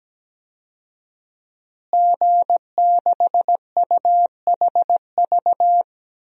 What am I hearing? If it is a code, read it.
Morse code, heard as a keyed tone, about 17 wpm, 700 Hz. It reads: G6UHV
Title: DXCC-100 Call Signs - 5 Characters - Repeated 3x - 17wpm